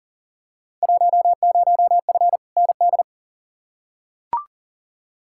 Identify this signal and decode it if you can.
Morse code, heard as a keyed tone, about 40 wpm, 700 Hz. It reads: 10F ND
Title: Field Day Exchanges Encore - 40wpm